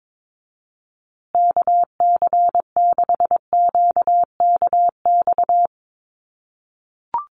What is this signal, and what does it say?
Morse code, heard as a keyed tone, about 22 wpm, 700 Hz. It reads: KC6QKX
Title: USA Call Signs Repeated 3x 22wpm